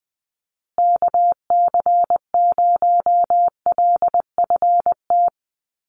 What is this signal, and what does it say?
Morse code, heard as a keyed tone, about 20 wpm, 700 Hz. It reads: KC0LFT
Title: USA Call Signs Repeated 2x 20wpm